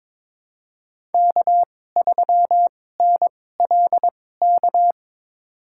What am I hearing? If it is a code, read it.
Morse code, heard as a keyed tone, about 22 wpm, spaced as for 16 wpm, 690 Hz. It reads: K3NLK
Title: USA Call Signs - 5 Characters - 2x Character Spacing - 22wpm